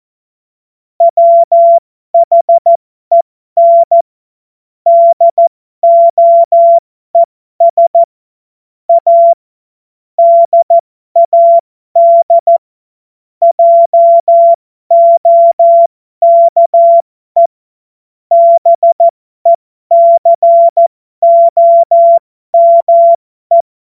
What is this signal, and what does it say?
Morse code, heard as a keyed tone, about 14 wpm, spaced as for 12 wpm, 680 Hz. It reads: WHEN DOES A DAD JOKE BECOME
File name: Morsekurs / Witze